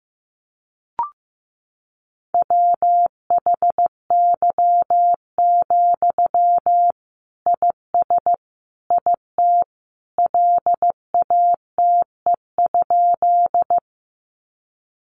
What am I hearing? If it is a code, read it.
Morse code, heard as a keyed tone, about 14 wpm, 700 Hz. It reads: WHY, IS IT LATE?
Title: Sentences from Top 300 Words 15wpm